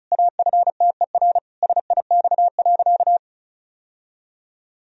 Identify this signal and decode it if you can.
Morse code, heard as a keyed tone, about 34 wpm, 700 Hz. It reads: AFTER SIX.
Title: Sentences from Top 1000 Words - Review Part 2 - 35wpm